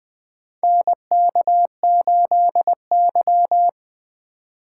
Morse code, heard as a keyed tone, about 20 wpm, 700 Hz. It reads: NK8Y
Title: USA Call Signs - 4 Characters - 3x Character Spacing - 20wpm